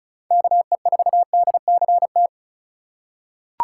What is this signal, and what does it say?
Morse code, heard as a keyed tone, about 34 wpm, 700 Hz. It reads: KE4DCT
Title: USA Call Signs - 6 Characters - 2x Character Spacing - 35wpm